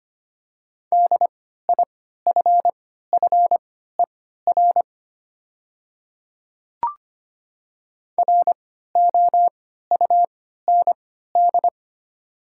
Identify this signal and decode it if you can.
Morse code, heard as a keyed tone, about 25 wpm, spaced as for 14 wpm, 700 Hz. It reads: DIFFER ROUND
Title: 3x Character Spacing - Top 200 Words - 25wpm